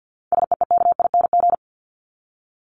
Morse code, heard as a keyed tone, about 50 wpm, 700 Hz. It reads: SEEKING
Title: ING Suffix - Top 100 Words with Base Words 50wpm